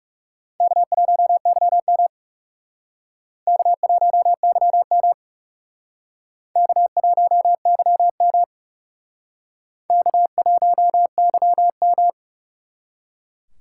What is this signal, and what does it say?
Morse code, heard as a keyed tone, about 38 wpm, 700 Hz. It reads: K1YM K1YM K1YM K1YM
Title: Speed Racing - USA 1x2 Call Signs 30wpm